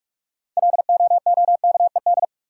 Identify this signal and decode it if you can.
Morse code, heard as a keyed tone, about 44 wpm, 700 Hz. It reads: LOOKED
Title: Sentences from Top 800 Words - New Word Drill - 45wpm